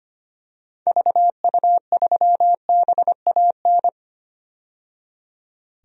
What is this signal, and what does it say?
Morse code, heard as a keyed tone, about 25 wpm, 700 Hz. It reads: VU3BAN
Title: DXCC-100 Call Signs - 6 Characters - 25wpm